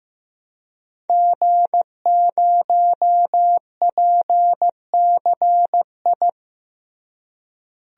Morse code, heard as a keyed tone, about 15 wpm, 700 Hz. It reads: G0PCI
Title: DXCC-100 Call Signs - 5 Characters - Repeated 3x - 15wpm